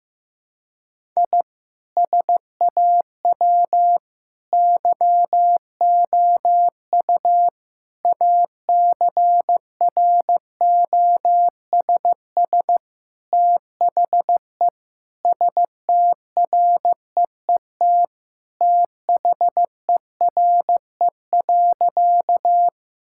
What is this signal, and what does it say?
Morse code, heard as a keyed tone, about 15 wpm, 700 Hz. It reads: I SAW YOU ACROSS THE STREET THERE.
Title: Sentences from Top 1000 Words - Review Part 5 - 15wpm